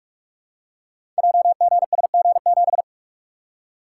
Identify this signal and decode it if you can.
Morse code, heard as a keyed tone, about 44 wpm, 700 Hz. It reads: JGSG7
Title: Code Groups - 5 Letter:Numbers - 45wpm